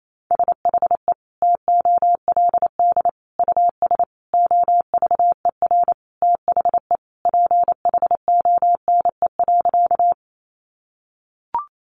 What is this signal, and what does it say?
Morse code, heard as a keyed tone, about 28 wpm, 700 Hz. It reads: SHE TOLD US OVER THE PHONE.
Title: Sentences from Top 1100 Words 28wpm